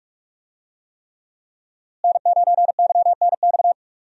Morse code, heard as a keyed tone, about 42 wpm, 700 Hz. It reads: N9YNX
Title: USA Call Signs - 5 Characters - Repeated 3x - 45wpm